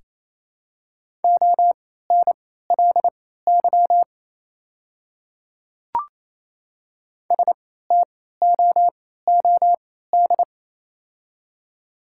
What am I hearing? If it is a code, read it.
Morse code, heard as a keyed tone, about 28 wpm, spaced as for 16 wpm, 700 Hz. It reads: ONLY STOOD
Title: 3x Character Spacing - Top 500 Words - 28wpm